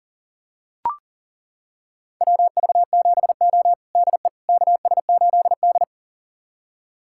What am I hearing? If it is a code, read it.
Morse code, heard as a keyed tone, about 40 wpm, 700 Hz. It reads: WV7O DE KS8D
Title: USA Callsign DE Callsign - 4 Characters - 40wpm